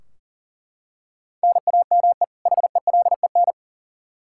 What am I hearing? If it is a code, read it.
Morse code, heard as a keyed tone, about 39 wpm, 700 Hz. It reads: NAME HELEN
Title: Practice QSO - Exchange 1 - Beginning - 40wpm